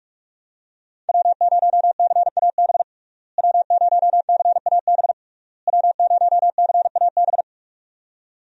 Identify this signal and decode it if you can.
Morse code, heard as a keyed tone, about 41 wpm, 700 Hz. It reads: W0CAB W0CAB W0CAB
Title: USA Call Signs - 5 Characters - Repeated 3x - 45wpm